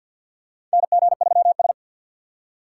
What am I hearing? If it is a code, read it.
Morse code, heard as a keyed tone, about 52 wpm, 700 Hz. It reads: NG3S
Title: USA Call Signs - 6x Character Spacing - 50wpm